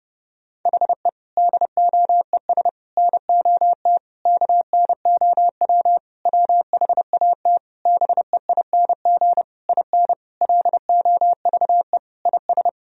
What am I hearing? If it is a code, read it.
Morse code, heard as a keyed tone, about 32 wpm, 700 Hz. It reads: HE DOES NOT KNOW WHAT BEING IN LOVE IS
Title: Sentences from Top 1000 Words - Review Part 3 - 30wpm